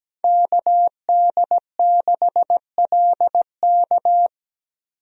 Morse code, heard as a keyed tone, about 17 wpm, 700 Hz. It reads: KD6LK
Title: USA Call Signs - 5 Characters - 17wpm